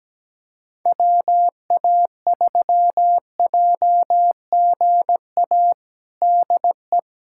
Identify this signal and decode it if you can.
Morse code, heard as a keyed tone, about 17 wpm, 700 Hz. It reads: WA3JGA DE
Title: USA Callsign DE Callsign 17wpm